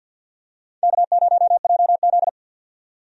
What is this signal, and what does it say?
Morse code, heard as a keyed tone, about 45 wpm, 700 Hz. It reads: K0JZ
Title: USA Call Signs - 4 Characters - 50wpm